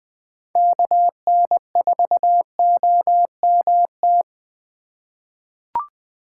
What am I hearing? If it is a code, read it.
Morse code, heard as a keyed tone, about 20 wpm, 700 Hz. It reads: KN4OMT
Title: USA Call Signs - 6 Characters - 4x Character Spacing - 20wpm